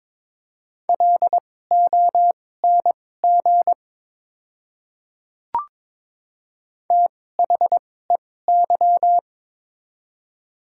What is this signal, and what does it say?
Morse code, heard as a keyed tone, about 22 wpm, spaced as for 16 wpm, 700 Hz. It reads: LONG THEY
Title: 2x Character Spacing - Top 100 Words - 22wpm